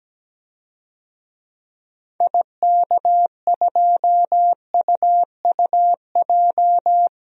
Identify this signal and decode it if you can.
Morse code, heard as a keyed tone, about 17 wpm, 700 Hz. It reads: IK2UUJ
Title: DXCC-100 Call Signs - 6 Characters - 17wpm